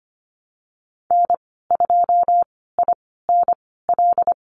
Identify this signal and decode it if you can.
Morse code, heard as a keyed tone, about 25 wpm, spaced as for 16 wpm, 700 Hz. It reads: N2INL
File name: Lesson 15g - Number 5 - Call Signs 25wpm